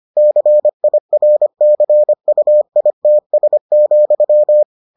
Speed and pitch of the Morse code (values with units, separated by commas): 25 wpm, 600 Hz